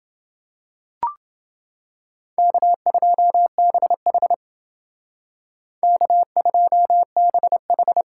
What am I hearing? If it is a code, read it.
Morse code, heard as a keyed tone, about 31 wpm, 700 Hz. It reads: K2BH K2BH